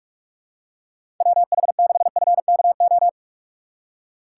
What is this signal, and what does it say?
Morse code, heard as a keyed tone, about 47 wpm, 700 Hz. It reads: WH6FKO
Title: USA Call Signs - 6 Characters - 4x Character Spacing - 45wpm